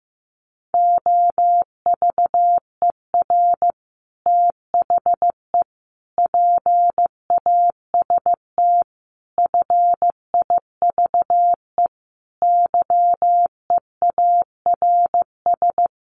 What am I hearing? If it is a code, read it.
Morse code, heard as a keyed tone, about 15 wpm, 700 Hz. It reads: OVER THE PAST FIVE YEARS
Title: Sets of 5 Words Encore 15wpm